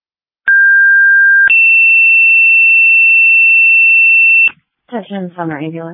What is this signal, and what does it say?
Two-tone page, heard as a one-second tone, then a three-second tone, about 1600 then 2700 Hz, with speech. It is heard from 0.5 to 4.5 s.